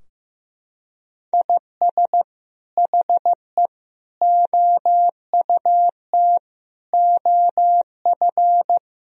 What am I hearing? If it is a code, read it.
Morse code, heard as a keyed tone, about 15 wpm, 700 Hz. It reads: IS HE OUT OF
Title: Sentences from Top 1000 Words - Review Part 4 - 15wpm